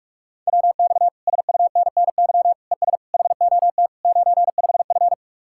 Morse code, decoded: WX SUNNY ES HOT 95F

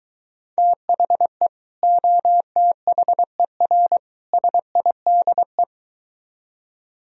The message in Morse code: THE OTHER SIDE

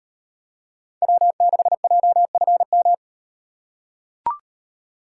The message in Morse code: W6JFM

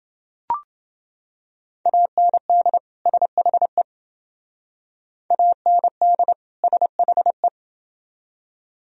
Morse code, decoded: AND SHE AND SHE

